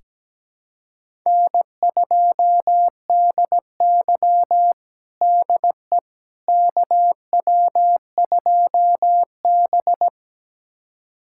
Morse code, heard as a keyed tone, about 17 wpm, 700 Hz. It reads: N2DY DE KW2B
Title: USA Callsign DE Callsign - 4 Characters - 17wpm